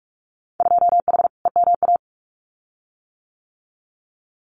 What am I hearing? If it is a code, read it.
Morse code, heard as a keyed tone, about 46 wpm, 700 Hz. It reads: 2H EMA
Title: Winter Field Day Exchanges 45wpm